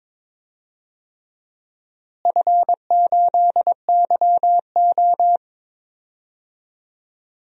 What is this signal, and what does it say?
Morse code, heard as a keyed tone, about 21 wpm, 700 Hz. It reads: F8YO
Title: DXCC-100 Call Signs - 4 Characters - Repeated 3x - 22wpm